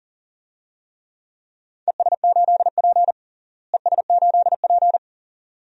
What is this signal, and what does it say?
Morse code, heard as a keyed tone, about 40 wpm, 700 Hz. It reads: ES8P ES8P
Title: DXCC-100 Call Signs - 4 Characters - Repeated 2x - 40wpm